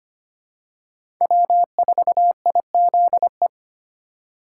Morse code, heard as a keyed tone, about 25 wpm, 700 Hz. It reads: W4IZE
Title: USA Call Signs - 3x Character Spacing - 25wpm